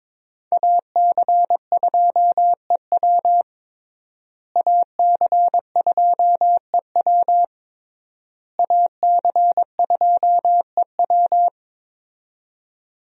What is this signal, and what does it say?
Morse code, heard as a keyed tone, about 22 wpm, 700 Hz. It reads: AC2EW AC2EW AC2EW
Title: USA Call Signs - 5 Characters - Repeated 3x - 22wpm